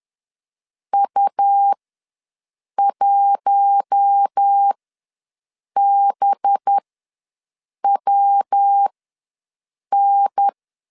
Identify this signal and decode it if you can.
Morse code, heard as a keyed tone, about 10 wpm, spaced as for 6 wpm, 800 Hz. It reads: U1BWN
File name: LEZIONI / 5 LBVC / 5 LB ws 10wpm 3616